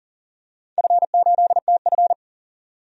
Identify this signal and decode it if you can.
Morse code, heard as a keyed tone, about 39 wpm, 700 Hz. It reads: F8TF
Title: DXCC-100 Call Signs - 4 Characters - 40wpm